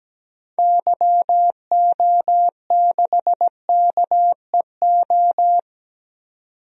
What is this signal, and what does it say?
Morse code, heard as a keyed tone, about 17 wpm, 700 Hz. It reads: YO6KEO
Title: DXCC-100 Call Signs - 6 Characters - 17wpm